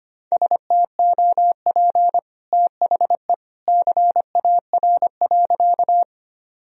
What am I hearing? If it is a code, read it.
Morse code, heard as a keyed tone, about 25 wpm, 700 Hz. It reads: STOP THE CAR.
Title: Sentences from Top 1000 Words - Review Part 1 - 25wpm